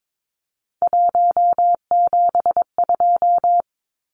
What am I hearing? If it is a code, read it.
Morse code, heard as a keyed tone, about 22 wpm, 700 Hz. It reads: 172
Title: Numbers - 3-Digit 22wpm